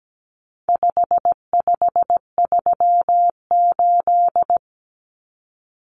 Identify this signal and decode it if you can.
Morse code, heard as a keyed tone, about 17 wpm, 700 Hz. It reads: 5538
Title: Common Cut Numbers - 4-Digit 17wpm